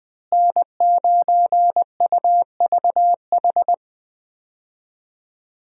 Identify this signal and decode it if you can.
Morse code, heard as a keyed tone, about 20 wpm, 700 Hz. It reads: N9UVH
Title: USA Call Signs - 5 Characters - Repeated 3x - 20wpm